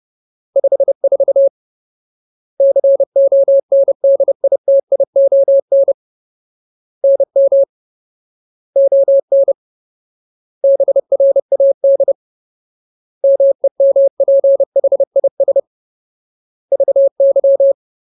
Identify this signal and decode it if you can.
Morse code, heard as a keyed tone, about 30 wpm, 550 Hz. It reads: 54 CONDITION NM ON BRAD MEMPHIS VY
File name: Ultimate Word List / 30WPM Ultimate